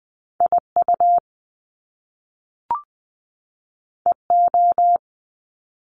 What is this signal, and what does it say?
Morse code, heard as a keyed tone, about 20 wpm, 700 Hz. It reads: IU EO
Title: Top Two Letter Combinations in Words 20wpm